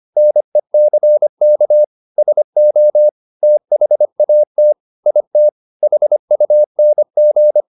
Morse code, heard as a keyed tone, about 25 wpm, 600 Hz. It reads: NECK SO THAT IT HUNG
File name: the old man and the sea part 1 25wpm